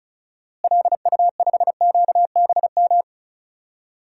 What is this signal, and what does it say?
Morse code, heard as a keyed tone, about 36 wpm, 700 Hz. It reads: LU5QBM
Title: DXCC-100 Call Signs - 6 Characters - Repeated 2x - 35wpm